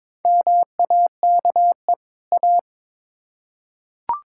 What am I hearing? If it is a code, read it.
Morse code, heard as a keyed tone, about 22 wpm, 700 Hz. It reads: MAKE A
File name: Build A Phrase - 2-Word Phrases - 22wpm